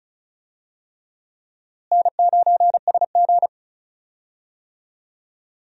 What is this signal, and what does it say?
Morse code, heard as a keyed tone, about 34 wpm, 700 Hz. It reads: N9SG
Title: USA Call Signs - 4 Characters - Repeated 3x - 35wpm